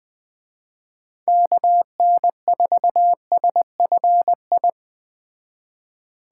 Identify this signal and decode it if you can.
Morse code, heard as a keyed tone, about 21 wpm, 700 Hz. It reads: KN4SFI
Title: Speed Racing - USA 2x3 Call Signs 17wpm